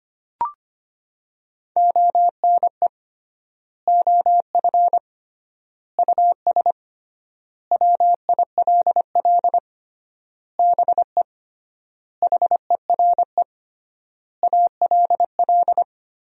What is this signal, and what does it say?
Morse code, heard as a keyed tone, about 25 wpm, 700 Hz. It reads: ONE OF US WILL BE HERE ALL
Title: 3x Word-Spacing - Sentences from Top 300 Words - 25wpm